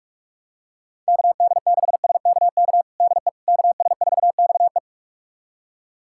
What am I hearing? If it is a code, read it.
Morse code, heard as a keyed tone, about 46 wpm, 700 Hz. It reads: KD6SKK DE KS4XE
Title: USA Callsign DE Callsign 45wpm